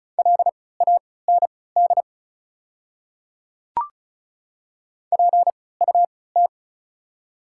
Morse code, heard as a keyed tone, about 34 wpm, spaced as for 20 wpm, 690 Hz. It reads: LAND PUT
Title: 3x Character Spacing - Top 300 Words - 35wpm